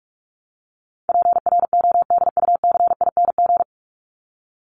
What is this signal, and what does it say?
Morse code, heard as a keyed tone, about 44 wpm, 700 Hz. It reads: PRODUCING
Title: ING Suffix - Top 100 Words with Base Words 45wpm